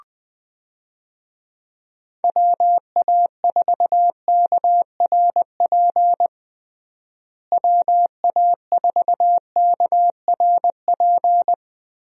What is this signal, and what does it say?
Morse code, heard as a keyed tone, about 20 wpm, 700 Hz. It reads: WA4KRP WA4KRP
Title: USA Call Signs - 6 Characters - Repeated 3x - 20wpm